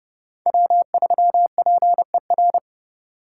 Morse code, heard as a keyed tone, about 33 wpm, 700 Hz. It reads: W3PER